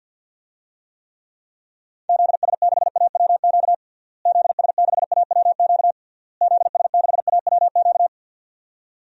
Morse code, heard as a keyed tone, about 48 wpm, 700 Hz. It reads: ZS6AWQ ZS6AWQ ZS6AWQ
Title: DXCC-100 Call Signs - 6 Characters - Repeated 3x - 50wpm